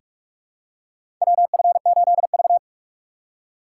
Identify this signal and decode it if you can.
Morse code, heard as a keyed tone, about 44 wpm, 700 Hz. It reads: WF8V
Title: USA Call Signs - 4 Characters - 3x Character Spacing - 45wpm